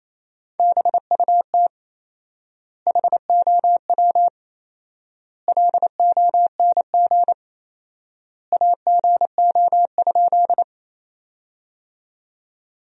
Morse code, heard as a keyed tone, about 27 wpm, 700 Hz. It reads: BUT HOW LONG AGO?
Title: 4x Word-Spacing - Sentences from Top 500 Words - 28wpm